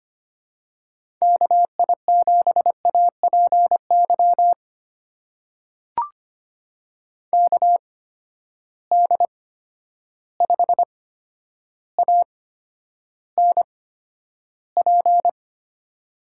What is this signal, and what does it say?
Morse code, heard as a keyed tone, about 25 wpm, spaced as for 11 wpm, 700 Hz. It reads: KI7APY KD5ANP
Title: USA Call Signs - 6 Characters - 8x Character Spacing - 25wpm